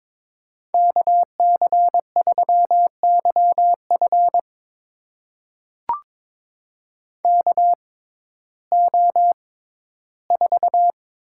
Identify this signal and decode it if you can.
Morse code, heard as a keyed tone, about 22 wpm, spaced as for 14 wpm, 700 Hz. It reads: KC3YF KO4
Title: USA Call Signs - 5 Characters - 6x Character Spacing - 22wpm